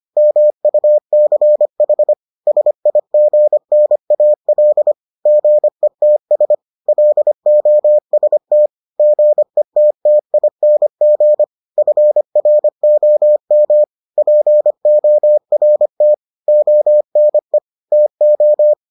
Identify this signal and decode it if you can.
Morse code, heard as a keyed tone, about 25 wpm, 600 Hz. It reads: MUCH SIGNAL GETS LOST GETTING FROM PORT ONE TO